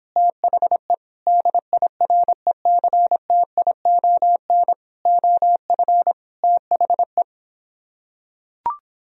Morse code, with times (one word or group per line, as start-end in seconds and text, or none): 0.2-0.9 THE
1.3-4.7 DIRECTION
5.1-6.1 OF
6.4-7.2 THE